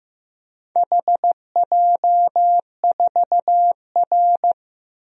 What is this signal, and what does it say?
Morse code, heard as a keyed tone, about 15 wpm, 700 Hz. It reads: HJ4R
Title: DXCC-100 Call Signs - 4 Characters - 15wpm